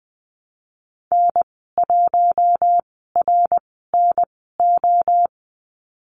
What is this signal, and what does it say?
Morse code, heard as a keyed tone, about 20 wpm, spaced as for 15 wpm, 700 Hz. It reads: N1RNO
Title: 055 Lesson 10g - Letter r - Call Signs 20wpm